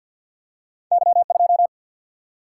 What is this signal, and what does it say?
Morse code, heard as a keyed tone, about 53 wpm, 700 Hz. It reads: Y2
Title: Single-Letter Single-Number 50wpm